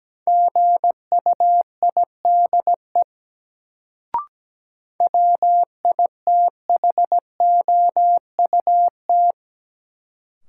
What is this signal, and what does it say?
Morse code, heard as a keyed tone, about 17 wpm, 700 Hz. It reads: GUIDE WITHOUT
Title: Sentences from Top 600 Words - New Word Drill - 17wpm